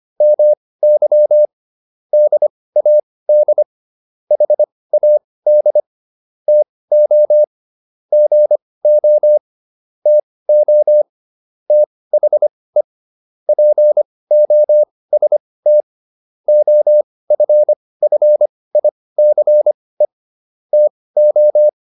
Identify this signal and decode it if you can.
Morse code, heard as a keyed tone, about 25 wpm, spaced as for 18 wpm, 600 Hz. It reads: MY DAD HAD TO GO TO THE POST OFFICE TO